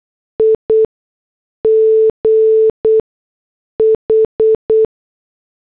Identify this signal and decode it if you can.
Morse code, heard as a keyed tone, about 8 wpm, spaced as for 6 wpm, 440 Hz. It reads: IGH
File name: morse / headlines